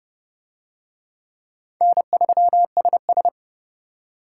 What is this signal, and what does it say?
Morse code, heard as a keyed tone, about 28 wpm, 700 Hz. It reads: N3SS